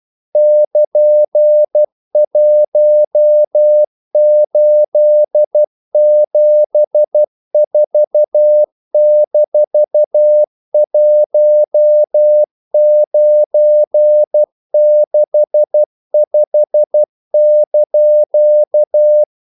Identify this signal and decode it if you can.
Morse code, heard as a keyed tone, about 12 wpm, 600 Hz. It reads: <KN>1874-1965)
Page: English quotations in CW (telegraphy - morse)